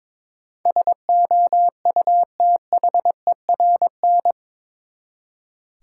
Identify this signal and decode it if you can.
Morse code, heard as a keyed tone, about 22 wpm, 700 Hz. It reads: SOUTHERN
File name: practice / 22wpm / Eight Letter Words - 22wpm